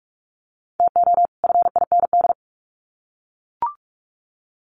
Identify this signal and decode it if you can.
Morse code, heard as a keyed tone, about 44 wpm, 700 Hz. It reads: TO FIND
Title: Sets of 2 Words 45wpm